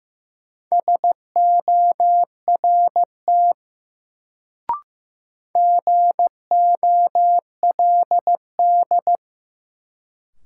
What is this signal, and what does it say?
Morse code, heard as a keyed tone, about 15 wpm, 700 Hz. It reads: SORT GOLD
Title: Sentences from Top 1000 Words - Review Part 3 - New Word Drill - 15wpm